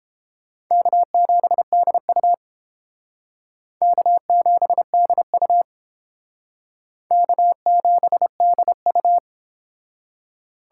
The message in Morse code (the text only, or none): K7DU K7DU K7DU